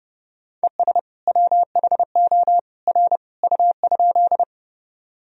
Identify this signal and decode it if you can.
Morse code, heard as a keyed tone, about 31 wpm, 700 Hz. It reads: ES WHO R U?
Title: Sentences from Top 100 Words and Abbreviations 30wpm